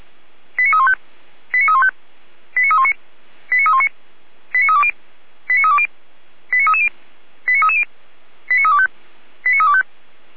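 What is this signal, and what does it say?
Five-tone selective call (ZVEI), heard as quick runs of five tones, tones of about 70 ms. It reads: 89315 89315 89319 89319 89320 89320 89330 89330 89325 89325
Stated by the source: ZVEI five-tone call received on an SDR by feuerwehr-orlamuende (fire brigade Orlamünde).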